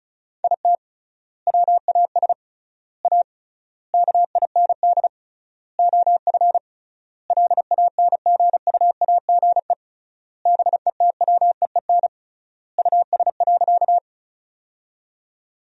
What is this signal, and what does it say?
Morse code, heard as a keyed tone, about 34 wpm, 700 Hz. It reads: IT WAS A KIND OF LANGUAGE BETWEEN US.
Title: 3x Word-Spacing - Sentences from Top 500 Words - 35wpm